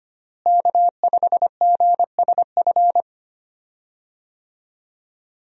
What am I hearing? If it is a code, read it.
Morse code, heard as a keyed tone, about 25 wpm, 700 Hz. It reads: K5GSF
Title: USA Call Signs - 5 Characters - Repeated 3x - 25wpm